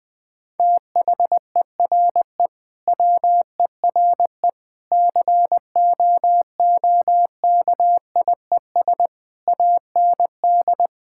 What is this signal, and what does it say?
Morse code, heard as a keyed tone, about 20 wpm, 700 Hz. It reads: THERE WERE COOKIES AND
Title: Sentences with 4-Word Phrases - Spoken Beginning - 20wpm